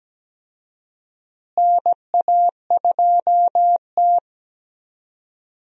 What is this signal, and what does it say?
Morse code, heard as a keyed tone, about 17 wpm, 700 Hz. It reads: NA2T
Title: USA Call Signs - 4 Characters - Repeated 2x - 17wpm